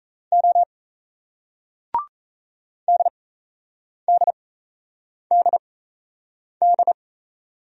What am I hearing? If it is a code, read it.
Morse code, heard as a keyed tone, about 38 wpm, spaced as for 8 wpm, 700 Hz. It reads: O DDDD